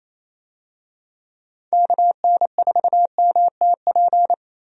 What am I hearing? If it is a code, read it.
Morse code, heard as a keyed tone, about 27 wpm, 700 Hz. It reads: KN4MTP
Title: USA Call Signs - 6 Characters - Repeated 3x - 28wpm